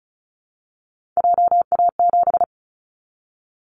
Morse code, heard as a keyed tone, about 36 wpm, 700 Hz. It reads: JA7
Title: Callsign Prefixes for Top 50 DXCC - 35wpm